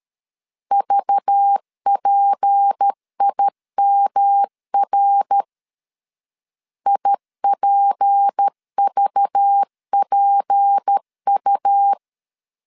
Morse code, heard as a keyed tone, about 13 wpm, 800 Hz. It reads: VPIMR IPVPU
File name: LEZIONI / 6 PKX0 / 6 P 12wpm 2033